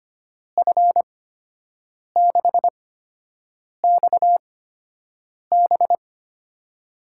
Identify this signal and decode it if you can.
Morse code, heard as a keyed tone, about 25 wpm, spaced as for 7 wpm, 700 Hz. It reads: F6XB
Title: USA Call Signs - 5 Characters - 8x Character Spacing - 25wpm